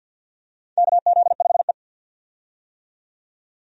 Morse code, heard as a keyed tone, about 51 wpm, 700 Hz. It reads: KZ5E
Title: USA Call Signs - 4 Characters - 50wpm